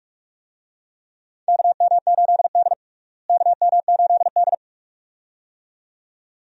Morse code, heard as a keyed tone, about 46 wpm, 700 Hz. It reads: KM8D KM8D
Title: USA Call Signs - 4 Characters - Repeated 2x - 45wpm